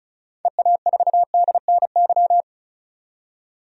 Morse code, heard as a keyed tone, about 36 wpm, 700 Hz. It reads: EA4DNY
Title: DXCC-100 Call Signs - 6 Characters - 35wpm